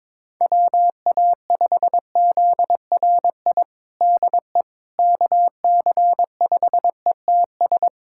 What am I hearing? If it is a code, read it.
Morse code, heard as a keyed tone, about 22 wpm, 700 Hz. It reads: WA5ZRI DE KC5ETS
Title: USA Callsign DE Callsign 22wpm